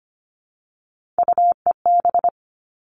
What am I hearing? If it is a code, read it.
Morse code, heard as a keyed tone, about 25 wpm, 700 Hz. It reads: UEB
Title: All Cut Numbers - 3-Digit 25wpm